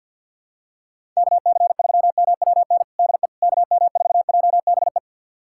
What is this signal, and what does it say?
Morse code, heard as a keyed tone, about 52 wpm, 700 Hz. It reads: KC3MWN DE KM4JBE
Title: USA Callsign DE Callsign - 6 Characters - 50wpm